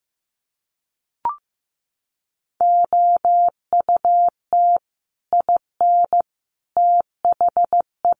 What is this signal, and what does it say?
Morse code, heard as a keyed tone, about 15 wpm, 700 Hz. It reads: OUT IN THE
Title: Sets of 3 Words 15wpm